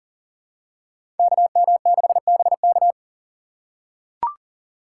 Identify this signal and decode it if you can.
Morse code, heard as a keyed tone, about 40 wpm, 700 Hz. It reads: KM6BK
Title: USA Call Signs - 5 Characters - 4x Character Spacing - 40wpm